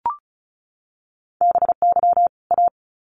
Morse code, heard as a keyed tone, about 34 wpm, 700 Hz. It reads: BY A
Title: Sets of 2 Words 35wpm